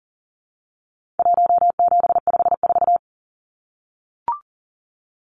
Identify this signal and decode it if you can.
Morse code, heard as a keyed tone, about 40 wpm, 700 Hz. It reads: 1754
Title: Numbers - Contest Style - 4-Digit 40wpm